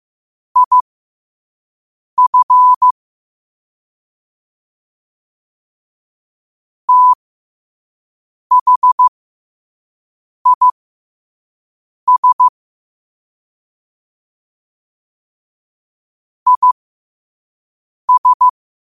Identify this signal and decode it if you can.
Morse code, heard as a keyed tone, about 15 wpm, spaced as for 5 wpm, 1000 Hz. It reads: IF THIS IS
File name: qotdfiles / 5wpm